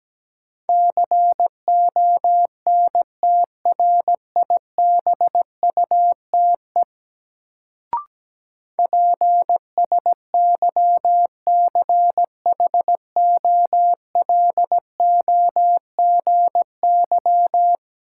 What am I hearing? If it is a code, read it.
Morse code, heard as a keyed tone, about 17 wpm, 700 Hz. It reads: CONTRIBUTE PSYCHOLOGY
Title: Sentences from Top 1400 Words - New Word Drill - 17wpm